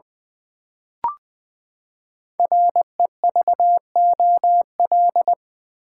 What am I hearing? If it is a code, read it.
Morse code, heard as a keyed tone, about 20 wpm, 700 Hz. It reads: REVOL
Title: Build A Word - 5 Letters At A Time - 10-Letter Words - 20wpm